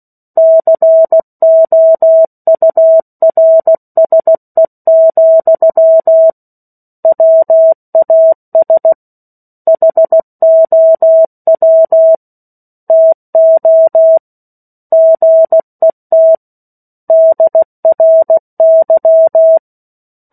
Morse code, heard as a keyed tone, about 16 wpm, 650 Hz. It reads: COURSE, WAS HOW TO GET DRY
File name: 03-Chapter-03 16 WPM